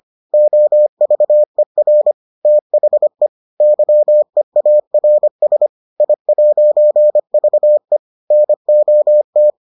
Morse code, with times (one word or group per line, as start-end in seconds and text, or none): 0.3-2.1 OVER
2.5-3.3 THE
3.6-5.7 YEARS
6.0-8.0 I'VE
8.3-9.5 NOT